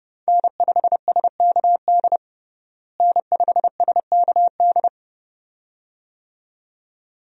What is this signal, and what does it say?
Morse code, heard as a keyed tone, about 30 wpm, 700 Hz. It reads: N5SKD N5SKD
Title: USA Call Signs - 5 Characters - Repeated 2x - 30wpm